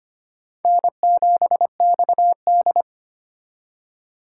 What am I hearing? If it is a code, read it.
Morse code, heard as a keyed tone, about 25 wpm, 700 Hz. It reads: N7XD
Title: USA Call Signs - 4 Characters - 2x Character Spacing - 25wpm